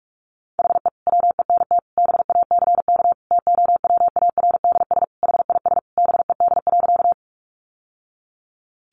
Morse code, heard as a keyed tone, about 44 wpm, 700 Hz. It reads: HE WENT BACK TOWARDS HIS BED.
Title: Sentences from Top 1100 Words 45wpm